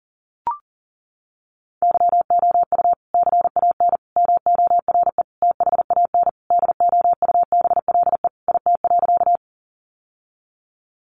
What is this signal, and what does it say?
Morse code, heard as a keyed tone, about 40 wpm, 700 Hz. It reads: YOU CAN MORE THAN DOUBLE IT.
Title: Sentences from Top 1400 Words 40wpm